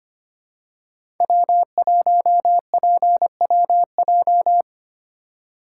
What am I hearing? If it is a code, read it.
Morse code, heard as a keyed tone, about 25 wpm, 700 Hz. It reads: W1PWJ
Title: USA Call Signs - 5 Characters - 2x Character Spacing - 25wpm